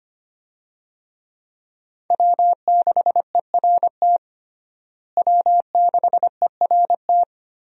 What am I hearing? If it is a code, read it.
Morse code, heard as a keyed tone, about 25 wpm, 700 Hz. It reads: W6ERT W6ERT
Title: USA Call Signs - 5 Characters - Repeated 2x - 25wpm